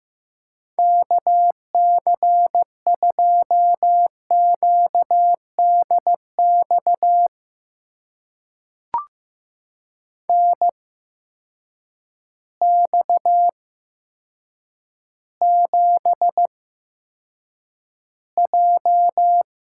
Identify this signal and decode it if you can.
Morse code, heard as a keyed tone, about 15 wpm, spaced as for 7 wpm, 700 Hz. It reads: KC2QDX NX7J
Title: USA Call Signs - 8x Character Spacing - 15wpm